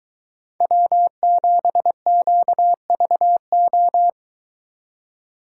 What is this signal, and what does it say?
Morse code, heard as a keyed tone, about 22 wpm, 700 Hz. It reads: W7QVO